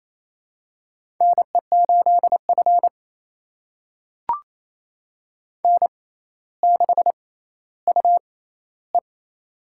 Morse code, heard as a keyed tone, about 28 wpm, spaced as for 14 wpm, 700 Hz. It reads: NE8F N6UE
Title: USA Call Signs - 4 Characters - 6x Character Spacing - 28wpm